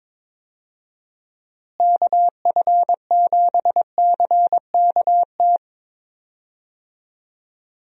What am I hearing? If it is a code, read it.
Morse code, heard as a keyed tone, about 22 wpm, 700 Hz. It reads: KF7CKT
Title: USA Call Signs - 6 Characters - Repeated 2x - 22wpm